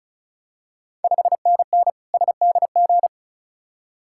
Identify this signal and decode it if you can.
Morse code, heard as a keyed tone, about 36 wpm, 700 Hz. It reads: 5NN SDG
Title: Ontario QSO Party Exchange 35wpm